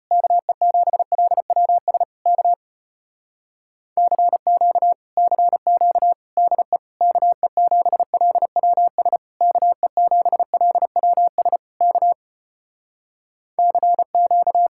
Morse code, heard as a keyed tone, about 33 wpm, 700 Hz. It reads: KE7LWS K CQ CQ DE KE7LWS KE7LWS K CQ